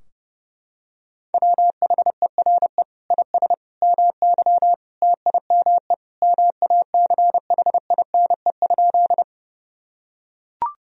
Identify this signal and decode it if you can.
Morse code, heard as a keyed tone, about 30 wpm, 700 Hz. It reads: WHERE IS MY TIME MACHINE?
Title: Sentences from Top 1000 Words - Review Part 2 - 30wpm